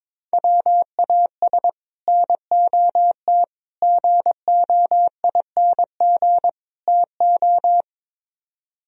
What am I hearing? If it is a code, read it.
Morse code, heard as a keyed tone, about 22 wpm, 700 Hz. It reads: WAS NOT GOING TO
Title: Speed-Racing - Sets of 4 Words 22wpm